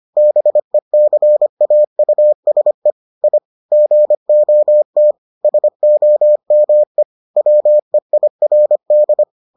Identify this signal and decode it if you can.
Morse code, heard as a keyed tone, about 25 wpm, 600 Hz. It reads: BECAUSE I GOT SOME WEIRD